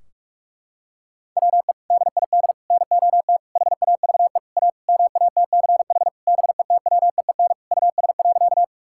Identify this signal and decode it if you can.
Morse code, heard as a keyed tone, about 46 wpm, 700 Hz. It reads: WE DID NOT HAVE A MATCH BETWEEN US.
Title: Sentences from Top 1600 Words 45wpm